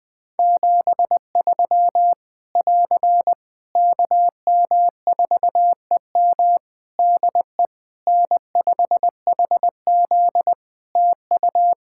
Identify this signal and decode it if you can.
Morse code, heard as a keyed tone, about 21 wpm, 700 Hz. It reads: 73 <AR> KM4EM DE N5HZ TU
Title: Practice QSO - Exchange 4 - Ending - 20wpm